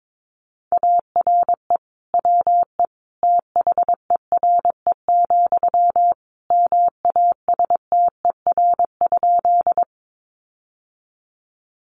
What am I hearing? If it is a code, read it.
Morse code, heard as a keyed tone, about 22 wpm, 700 Hz. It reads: ARE WE THERE, MASTER?
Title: Sentences from Top 1300 Words 22wpm